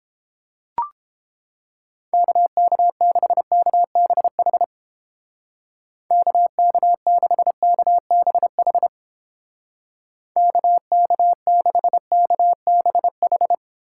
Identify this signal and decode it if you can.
Morse code, heard as a keyed tone, about 30 wpm, 700 Hz. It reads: KK6KBH KK6KBH KK6KBH